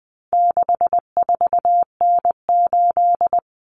Morse code, heard as a keyed tone, about 20 wpm, 700 Hz. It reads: 64N8
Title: Common Cut Numbers - 4-Digit 20wpm